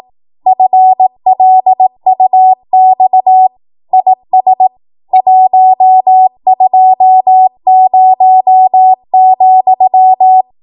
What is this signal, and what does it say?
Morse code, heard as a keyed tone, about 18 wpm, 750 Hz. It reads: FLUX IS 120,
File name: Morse / 18WPM